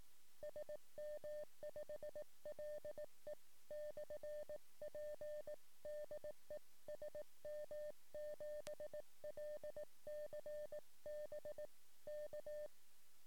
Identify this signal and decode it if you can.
Morse code, heard as a keyed tone, about 18 wpm, 600 Hz. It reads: SM5LE/P DE SM7LCB K
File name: sm7lcb / 08 sm5le-p jo98ax 23cm